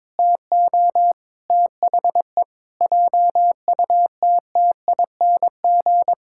Morse code, heard as a keyed tone, about 22 wpm, 700 Hz. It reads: TO THE JUTTING
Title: Pangram Challenge v2 - 22wpm